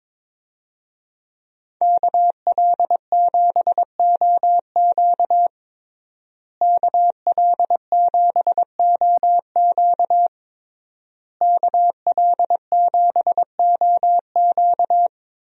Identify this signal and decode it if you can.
Morse code, heard as a keyed tone, about 22 wpm, 700 Hz. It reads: KL7OQ KL7OQ KL7OQ
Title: DXCC-100 Call Signs - 5 Characters - Repeated 3x - 22wpm